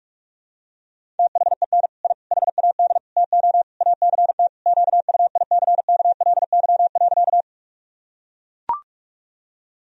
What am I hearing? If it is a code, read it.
Morse code, heard as a keyed tone, about 42 wpm, 700 Hz. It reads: THEN I HAD TO ACT QUICKLY.
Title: Sentences from Top 1000 Words - Review Part 1 - 45wpm